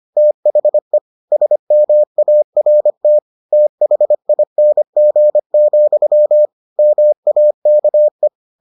Morse code, heard as a keyed tone, about 25 wpm, 600 Hz. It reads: THE SMART THING, MAKE